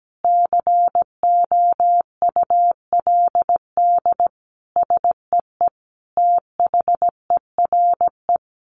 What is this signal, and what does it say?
Morse code, heard as a keyed tone, about 17 wpm, 700 Hz. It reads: COULD SEE THERE
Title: Sentences from Top 500 Words 17wpm